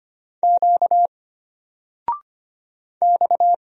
Morse code, heard as a keyed tone, about 25 wpm, 700 Hz. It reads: Q X
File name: Single-Letter-Number-25wpm